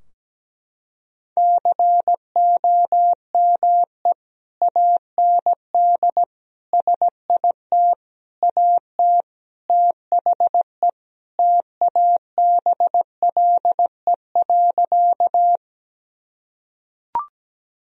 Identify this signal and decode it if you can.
Morse code, heard as a keyed tone, about 17 wpm, 700 Hz. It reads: COME AND SIT AT THE TABLE.
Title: Sentences from Top 1000 Words - Review Part 2 - 17wpm